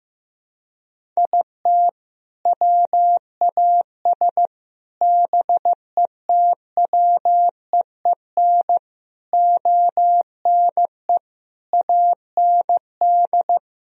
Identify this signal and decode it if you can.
Morse code, heard as a keyed tone, about 16 wpm, 700 Hz. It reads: IT WAS BETWEEN ONE AND
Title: Sentences from Top 1000 Words - Review Part 1 - 15wpm